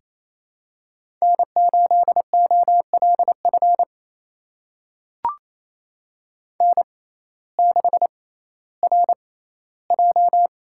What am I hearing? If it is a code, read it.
Morse code, heard as a keyed tone, about 28 wpm, spaced as for 15 wpm, 700 Hz. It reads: N8OLF N6RJ
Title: USA Call Signs - 5 Characters - 6x Character Spacing - 28wpm